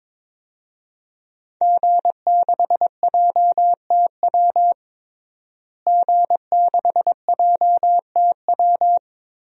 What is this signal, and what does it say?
Morse code, heard as a keyed tone, about 22 wpm, 700 Hz. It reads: G6JTW G6JTW
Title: DXCC-100 Call Signs - 5 Characters - Repeated 3x - 22wpm